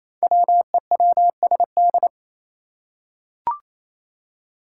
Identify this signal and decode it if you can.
Morse code, heard as a keyed tone, about 28 wpm, 700 Hz. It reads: WEWSD
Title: Five Challenging Letter-Number Combinations 28wpm